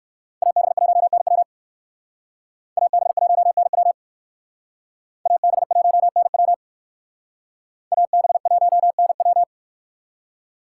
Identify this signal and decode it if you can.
Morse code, heard as a keyed tone, about 57 wpm, 700 Hz. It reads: AB1NW AB1NW AB1NW AB1NW